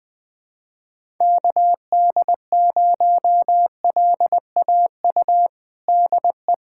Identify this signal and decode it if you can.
Morse code, heard as a keyed tone, about 20 wpm, 700 Hz. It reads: KD0LAU DE
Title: USA Callsign DE Callsign 20wpm